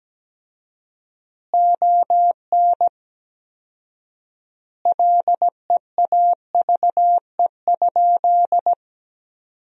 Morse code, heard as a keyed tone, about 17 wpm, 700 Hz. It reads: ON LEAVE?
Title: 4x Word-Spacing - Sentences from Top 400 Words - 17wpm